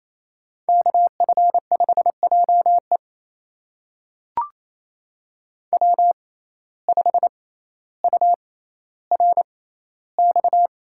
Morse code, heard as a keyed tone, about 28 wpm, spaced as for 14 wpm, 700 Hz. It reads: KF5JE W5URX
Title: USA Call Signs - 5 Characters - 6x Character Spacing - 28wpm